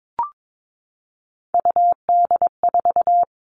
Morse code, heard as a keyed tone, about 22 wpm, 700 Hz. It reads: UD4
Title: All Cut Numbers - 3-Digit 22wpm